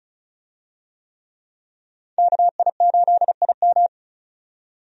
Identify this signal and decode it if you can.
Morse code, heard as a keyed tone, about 36 wpm, 700 Hz. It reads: KI8IM
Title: USA Call Signs - 5 Characters - 35wpm